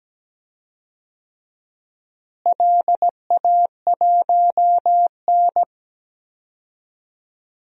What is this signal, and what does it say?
Morse code, heard as a keyed tone, about 17 wpm, 700 Hz. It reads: LA1N
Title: DXCC-100 Call Signs - 4 Characters - Repeated 2x - 17wpm